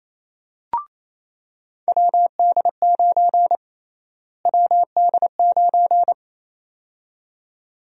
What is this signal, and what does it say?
Morse code, heard as a keyed tone, about 28 wpm, 700 Hz. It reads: WD9 WD9
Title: USA Call Sign Prefixes - Repeated 2x - 28wpm